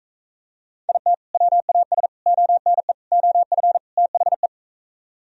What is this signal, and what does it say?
Morse code, heard as a keyed tone, about 44 wpm, 700 Hz. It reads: IT WAS ONE OF THE